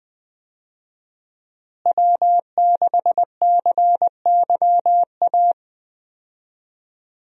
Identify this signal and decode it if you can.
Morse code, heard as a keyed tone, about 20 wpm, 700 Hz. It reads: W6CYA
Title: USA Call Signs - 5 Characters - Repeated 2x - 20wpm